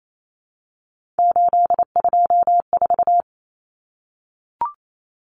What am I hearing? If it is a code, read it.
Morse code, heard as a keyed tone, about 28 wpm, 700 Hz. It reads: 824
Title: Common Cut Numbers - 3-Digit 28wpm